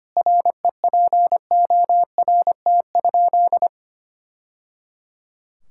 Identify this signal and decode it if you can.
Morse code, heard as a keyed tone, about 25 wpm, 700 Hz. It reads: REPORT?
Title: Sentences from Top 1000 Words - Review Part 6 - 25wpm